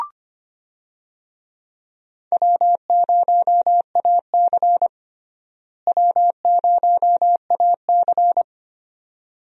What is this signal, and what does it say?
Morse code, heard as a keyed tone, about 25 wpm, 700 Hz. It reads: W0AC W0AC
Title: USA Call Signs - 4 Characters - Repeated 2x - 25wpm